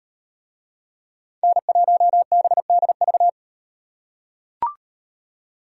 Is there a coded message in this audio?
Yes: Morse code from 1.4 to 3.3 s, heard as a keyed tone, about 35 wpm, 700 Hz. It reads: N1BDV